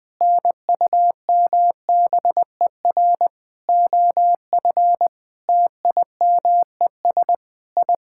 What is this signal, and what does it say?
Morse code, heard as a keyed tone, about 20 wpm, 700 Hz. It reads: NUMBER OF TIMES I
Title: Sentences with 9-Word Phrases - Spoken Beginning - 20wpm